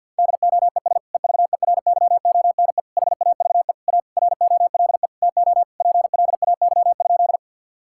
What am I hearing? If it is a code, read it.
Morse code, heard as a keyed tone, about 48 wpm, 700 Hz. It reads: DOES EVERYONE HAVE A ROLE TO PLAY?
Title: Sentences from Top 1000 Words - Review Part 6 - 50wpm